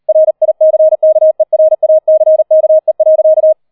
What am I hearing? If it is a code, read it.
Morse code, heard as a keyed tone, about 38 wpm, 630 Hz. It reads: RICKERACKE.